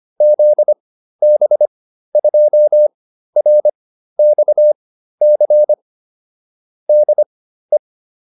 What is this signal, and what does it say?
Morse code, heard as a keyed tone, about 25 wpm, spaced as for 13 wpm, 600 Hz. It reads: ZB2RXC DE